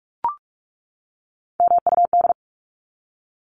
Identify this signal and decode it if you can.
Morse code, heard as a keyed tone, about 44 wpm, 700 Hz. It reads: MUD
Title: Top 1751-2000 Words 45wpm